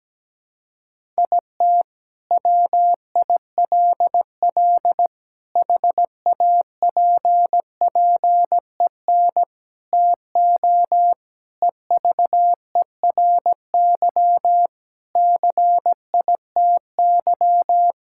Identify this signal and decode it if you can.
Morse code, heard as a keyed tone, about 17 wpm, 700 Hz. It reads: IT WILL HAPPEN TO EVERY CITY
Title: Sentences from Top 1000 Words - Review Part 6 - 17wpm